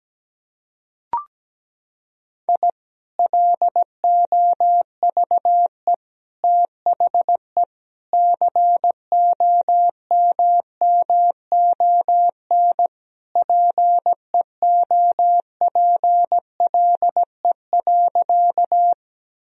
Morse code, heard as a keyed tone, about 17 wpm, 700 Hz. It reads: I LOVE THE COMMON PEOPLE.
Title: Sentences from Top 1000 Words - Review Part 3 - 17wpm